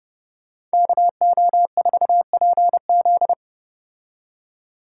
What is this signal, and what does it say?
Morse code, heard as a keyed tone, about 34 wpm, 700 Hz. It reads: KO4PZ